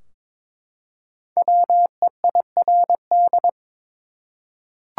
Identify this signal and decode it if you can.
Morse code, heard as a keyed tone, about 22 wpm, 700 Hz. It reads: WEIRD
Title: Five Letter Words - 22wpm